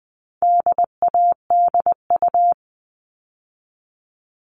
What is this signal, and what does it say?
Morse code, heard as a keyed tone, about 20 wpm, 700 Hz. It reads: DADU
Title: All Cut Numbers - 4-Digit 20wpm